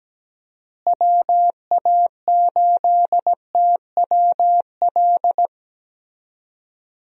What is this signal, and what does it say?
Morse code, heard as a keyed tone, about 17 wpm, 700 Hz. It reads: WA8TWL
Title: USA Call Signs - 6 Characters - Repeated 3x - 17wpm